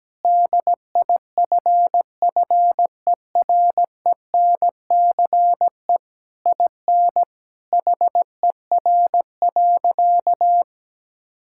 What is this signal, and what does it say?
Morse code, heard as a keyed tone, about 17 wpm, 700 Hz. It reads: DIFFERENCE IN HER.
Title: Sentences from Top 1000 Words - Review Part 5 - 17wpm